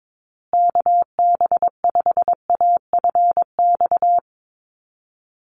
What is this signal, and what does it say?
Morse code, heard as a keyed tone, about 22 wpm, 700 Hz. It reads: KB5AFX
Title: USA Call Signs Repeated 3x 22wpm